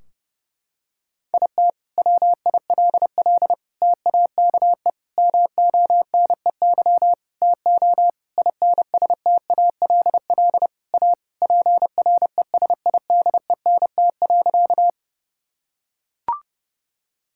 Morse code, heard as a keyed tone, about 30 wpm, 700 Hz. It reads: IT WILL TAKE MONEY TO INSTALL A PRESIDENT.
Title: Sentences from Top 1600 Words 30wpm